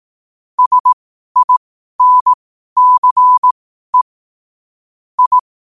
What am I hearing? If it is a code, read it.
Morse code, heard as a keyed tone, about 18 wpm, spaced as for 12 wpm, 1000 Hz. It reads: SINCE I
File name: qotdfiles / 12wpm